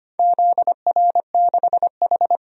Morse code, heard as a keyed tone, about 25 wpm, 700 Hz. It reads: ZR6H
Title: DXCC-100 Call Signs - 4 Characters - 25wpm